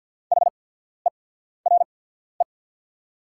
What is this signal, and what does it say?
Morse code, heard as a keyed tone, about 48 wpm, spaced as for 14 wpm, 700 Hz. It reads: HERE
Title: 8x Character Spacing - Top 200 Words - 50wpm